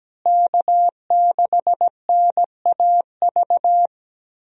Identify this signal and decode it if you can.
Morse code, heard as a keyed tone, about 17 wpm, 700 Hz. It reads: K6NAV
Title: USA Call Signs - 5 Characters - Repeated 3x - 17wpm